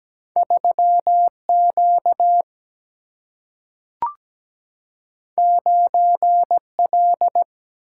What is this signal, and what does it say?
Morse code, heard as a keyed tone, about 17 wpm, 700 Hz. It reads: 3Q 9L
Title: Single-Letter Single-Number 17wpm